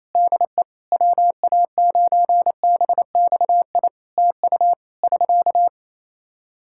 Morse code, heard as a keyed tone, about 28 wpm, 700 Hz. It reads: DE WA9BXI TU <SK>
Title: K1USN SST QSO Practice 28wpm